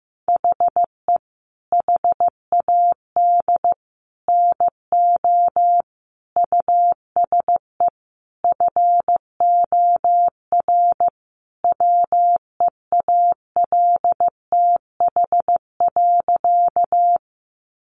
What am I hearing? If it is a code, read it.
Morse code, heard as a keyed tone, about 14 wpm, 700 Hz. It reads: HE HAD NO USE FOR WEALTH.
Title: Sentences from Top 1200 Words 15wpm